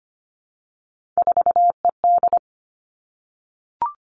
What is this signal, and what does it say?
Morse code, heard as a keyed tone, about 25 wpm, 700 Hz. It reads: 4ED
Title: Three Challenging Letter-Number Combinations 25wpm